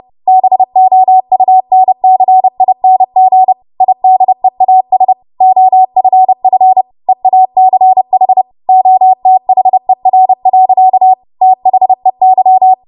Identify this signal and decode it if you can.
Morse code, heard as a keyed tone, about 30 wpm, 750 Hz. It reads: BOUNCING IDEAS OFF EACH OTHER. THEY
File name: Morse / 30WPM